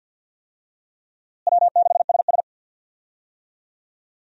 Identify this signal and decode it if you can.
Morse code, heard as a keyed tone, about 48 wpm, 700 Hz. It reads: W6SS
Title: USA Call Signs - 4 Characters - Repeated 2x - 50wpm